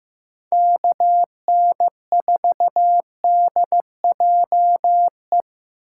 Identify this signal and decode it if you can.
Morse code, heard as a keyed tone, about 16 wpm, 700 Hz. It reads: KN4DJE
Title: USA Call Signs - 2x Character Spacing - 15wpm